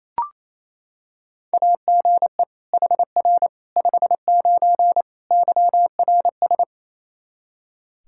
Morse code, heard as a keyed tone, about 27 wpm, 700 Hz. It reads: AGE HR 59 YRS
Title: K1USN SST QSO Practice 28wpm